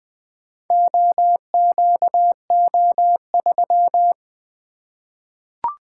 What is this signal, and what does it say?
Morse code, heard as a keyed tone, about 20 wpm, 700 Hz. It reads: OQO3
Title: Four Letter-Number Combinations - 20wpm